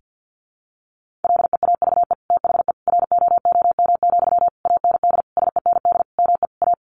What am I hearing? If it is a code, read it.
Morse code, heard as a keyed tone, about 50 wpm, 700 Hz. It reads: LEAVE THE ROOM, AND SEND ME A
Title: Sentences from Top 1500 Words 50wpm